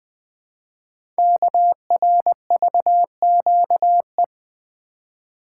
Morse code, heard as a keyed tone, about 20 wpm, 700 Hz. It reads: KRVQE
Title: Code Groups - 5 Letters - 20wpm